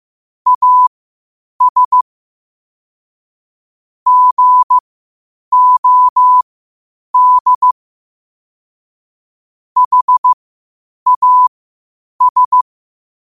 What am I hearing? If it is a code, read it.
Morse code, heard as a keyed tone, about 15 wpm, spaced as for 8 wpm, 1000 Hz. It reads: AS GOD HAS
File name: qotdfiles / 8wpm